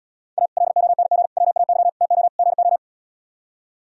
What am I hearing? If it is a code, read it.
Morse code, heard as a keyed tone, about 77 wpm, 700 Hz. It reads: A LONG WAY TO GO